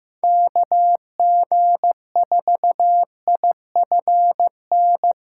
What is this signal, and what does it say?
Morse code, heard as a keyed tone, about 15 wpm, 700 Hz. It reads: KG4IFN
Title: USA Call Signs - 8x Character Spacing - 15wpm